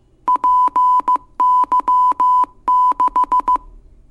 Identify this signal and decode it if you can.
Morse code, heard as a keyed tone, about 15 wpm, 1000 Hz. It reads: PY6